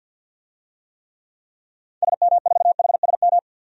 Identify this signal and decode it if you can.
Morse code, heard as a keyed tone, about 51 wpm, 700 Hz. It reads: SM4HSM
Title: DXCC-100 Call Signs - 6 Characters - 50wpm